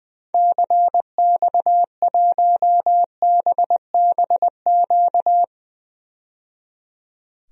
Morse code, heard as a keyed tone, about 20 wpm, 700 Hz. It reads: CX1BBQ
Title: DXCC-100 Call Signs - 6 Characters - 20wpm